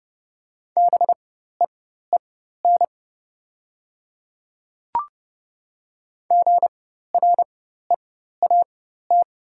Morse code, heard as a keyed tone, about 30 wpm, spaced as for 14 wpm, 700 Hz. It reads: BEEN GREAT
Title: 4x Character Spacing - Top 200 Words - 30wpm